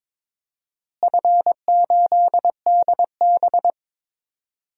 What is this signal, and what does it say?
Morse code, heard as a keyed tone, about 22 wpm, 700 Hz. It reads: F8DB
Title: DXCC-100 Call Signs - 4 Characters - Repeated 3x - 22wpm